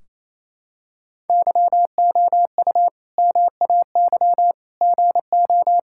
Morse code, heard as a keyed tone, about 28 wpm, 700 Hz. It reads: YOU MAY GO
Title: Sentences from Top 1600 Words 28wpm